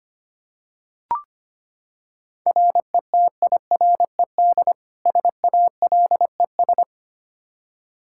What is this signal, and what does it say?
Morse code, heard as a keyed tone, about 25 wpm, 700 Hz. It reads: RETIRED SALES
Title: Practice QSO - Exchange 3 - More Info - 25wpm